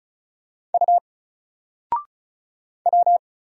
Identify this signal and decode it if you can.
Morse code, heard as a keyed tone, about 35 wpm, 700 Hz. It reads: U W
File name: Single-Letter-Number-35wpm